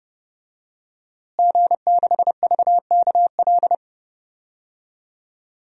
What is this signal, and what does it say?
Morse code, heard as a keyed tone, about 30 wpm, 700 Hz. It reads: G6VKL
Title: DXCC-100 Call Signs - 5 Characters - Repeated 3x - 30wpm